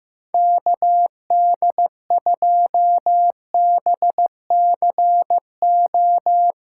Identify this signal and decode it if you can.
Morse code, heard as a keyed tone, about 17 wpm, 700 Hz. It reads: KD2BCO
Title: Speed Racing - USA 2x3 Call Signs 15wpm